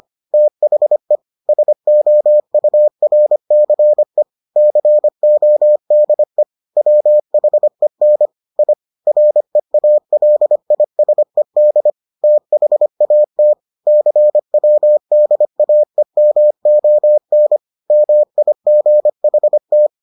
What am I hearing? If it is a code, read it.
Morse code, heard as a keyed tone, about 25 wpm, 600 Hz. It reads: THE SOURCE CODE WHEN I REALISED THAT CWDAEMON MIGHT